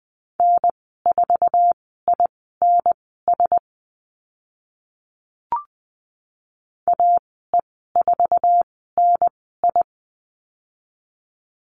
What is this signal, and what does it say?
Morse code, heard as a keyed tone, about 20 wpm, spaced as for 14 wpm, 700 Hz. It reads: N4INS AE4NI
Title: 055 Lesson 10g - Letter r - Call Signs 20wpm